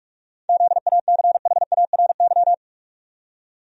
Morse code, heard as a keyed tone, about 46 wpm, 700 Hz. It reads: ZACHARY
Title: USA Common Names - 45wpm